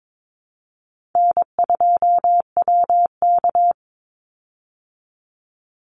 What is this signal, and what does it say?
Morse code, heard as a keyed tone, about 22 wpm, 700 Hz. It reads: N2WK
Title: Easy Call Signs Repeated 2x - 22wpm